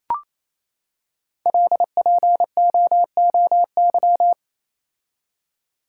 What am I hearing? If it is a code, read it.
Morse code, heard as a keyed tone, about 28 wpm, 700 Hz. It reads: LPOOY
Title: Code Groups - 5 Letters - 28wpm